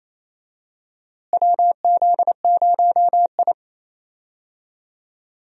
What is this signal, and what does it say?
Morse code, heard as a keyed tone, about 28 wpm, 700 Hz. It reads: WZ0I
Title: USA Call Signs - 4 Characters - Repeated 3x - 28wpm